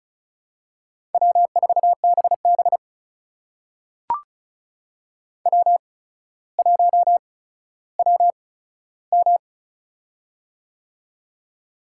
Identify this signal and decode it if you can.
Morse code, heard as a keyed tone, about 36 wpm, spaced as for 15 wpm, 700 Hz. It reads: W4BB W1WM
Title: USA Call Signs - 8x Character Spacing - 35wpm